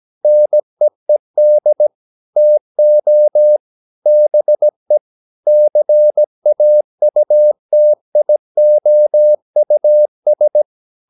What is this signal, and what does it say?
Morse code, heard as a keyed tone, about 17 wpm, 600 Hz. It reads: NEED TO BE CAUTIOUS
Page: Romanen Never af Ken Follet på engelsk.